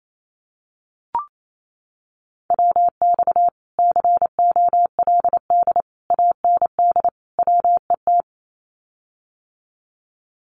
Morse code, heard as a keyed tone, about 28 wpm, 700 Hz. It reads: WX COLD AND WET
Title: Common QSO Phrases Encore 28wpm